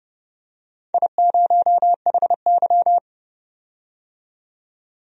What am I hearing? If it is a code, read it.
Morse code, heard as a keyed tone, about 30 wpm, 700 Hz. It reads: I0HY
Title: DXCC-100 Call Signs - 4 Characters - 30wpm